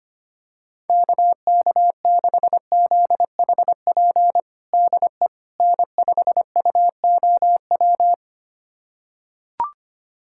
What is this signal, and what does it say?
Morse code, heard as a keyed tone, about 25 wpm, 700 Hz. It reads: KK6ZHP DE N5UOW
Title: USA Callsign DE Callsign 25wpm